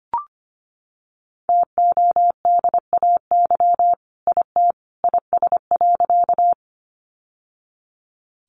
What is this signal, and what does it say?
Morse code, heard as a keyed tone, about 25 wpm, 700 Hz. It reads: TODAY IT IS.
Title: Sentences from Top 600 Words 25wpm